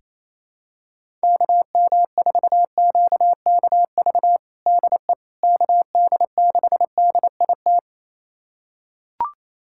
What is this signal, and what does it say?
Morse code, heard as a keyed tone, about 28 wpm, 700 Hz. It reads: KM4QKV DE KD6DIT
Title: USA Callsign DE Callsign - 6 Characters - 28wpm